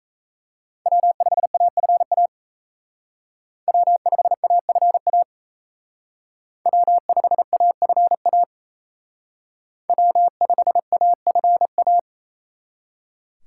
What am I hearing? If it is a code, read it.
Morse code, heard as a keyed tone, about 36 wpm, 700 Hz. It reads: W5AFA W5AFA W5AFA W5AFA